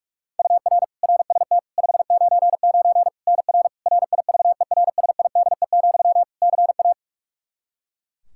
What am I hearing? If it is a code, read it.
Morse code, heard as a keyed tone, about 42 wpm, 700 Hz. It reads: UR RST 599 NR RIVERSIDE, CA
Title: Practice QSO - Exchange 1 - Beginning - 45wpm